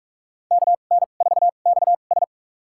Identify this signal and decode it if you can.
Morse code, heard as a keyed tone, about 46 wpm, spaced as for 32 wpm, 690 Hz. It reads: KN4XS
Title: USA Call Signs - 5 Characters - 2x Character Spacing - 45wpm